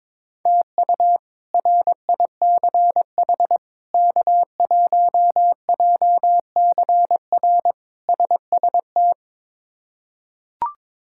Morse code, heard as a keyed tone, about 22 wpm, 700 Hz. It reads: TU RICH K1JCR SST
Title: K1USN SST QSO Practice - 22wpm